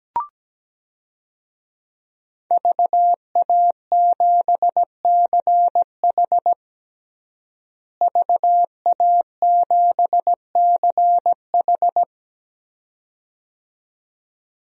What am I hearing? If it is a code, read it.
Morse code, heard as a keyed tone, about 17 wpm, 700 Hz. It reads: VA7CH VA7CH
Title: DXCC-100 Call Signs - 5 Characters - Repeated 2x - 17wpm